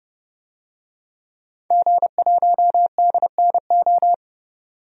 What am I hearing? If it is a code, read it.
Morse code, heard as a keyed tone, about 30 wpm, 700 Hz. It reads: G1DNO
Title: DXCC-100 Call Signs - 5 Characters - 30wpm